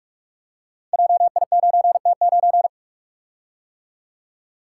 Morse code, heard as a keyed tone, about 44 wpm, 700 Hz. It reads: JI9T9
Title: Code Groups - 5 Letter:Numbers - 45wpm